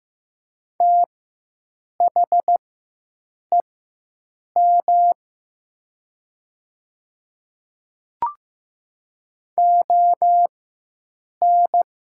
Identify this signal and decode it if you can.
Morse code, heard as a keyed tone, about 15 wpm, spaced as for 7 wpm, 700 Hz. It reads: THEM ON
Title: 4x Character Spacing - Top 100 Words - 15wpm